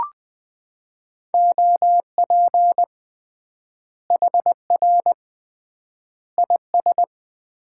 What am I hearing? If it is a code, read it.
Morse code, heard as a keyed tone, about 21 wpm, 700 Hz. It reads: OP HR IS
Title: Common QSO Phrases - 3x Word-Spacing - 20wpm